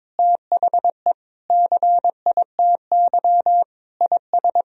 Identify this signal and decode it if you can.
Morse code, heard as a keyed tone, about 22 wpm, 700 Hz. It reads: THE CITY IS
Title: Sentences from Top 1000 Words - Review Part 4 - 22wpm